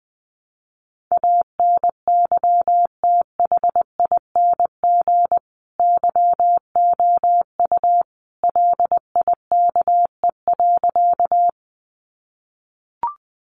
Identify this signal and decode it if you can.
Morse code, heard as a keyed tone, about 20 wpm, 700 Hz. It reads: ANYTHING YOU LIKE.
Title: Sentences from Top 1000 Words 20wpm